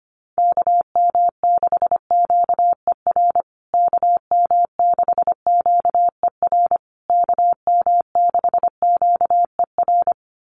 Morse code, heard as a keyed tone, about 25 wpm, 700 Hz. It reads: KM6QER KM6QER KM6QER
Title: USA Call Signs Repeated 3x 25wpm